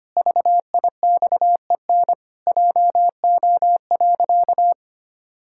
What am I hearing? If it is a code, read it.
Morse code, heard as a keyed tone, about 25 wpm, 700 Hz. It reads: VIXEN JO.
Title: Pangram Challenge v2 - 25wpm